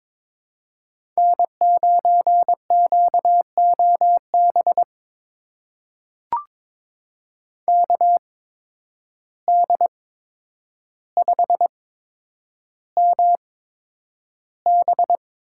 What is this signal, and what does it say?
Morse code, heard as a keyed tone, about 23 wpm, spaced as for 10 wpm, 700 Hz. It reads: N9QOB KD5MB
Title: USA Call Signs - 5 Characters - 8x Character Spacing - 22wpm